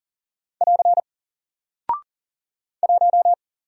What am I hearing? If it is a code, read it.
Morse code, heard as a keyed tone, about 40 wpm, 700 Hz. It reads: <AR> 1
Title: Single Character (Letter:Number:Punctuation:Prosign) 40wpm